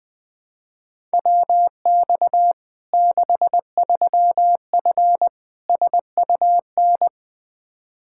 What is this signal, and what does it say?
Morse code, heard as a keyed tone, about 21 wpm, 700 Hz. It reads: WX 63F SUN
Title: Practice QSO - Exchange 2 - Info - 20wpm